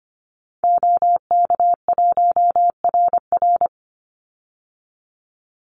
Morse code, heard as a keyed tone, about 25 wpm, 700 Hz. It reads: OK1RR
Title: Call Signs Repeated 2x - 25wpm